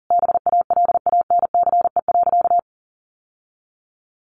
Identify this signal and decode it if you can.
Morse code, heard as a keyed tone, about 40 wpm, 700 Hz. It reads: BALANCE.
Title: Sentences from Top 1200 Words 40wpm